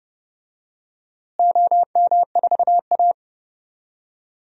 Morse code, heard as a keyed tone, about 32 wpm, 700 Hz. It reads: OM4A
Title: DXCC-100 Call Signs - 4 Characters - Repeated 2x - 30wpm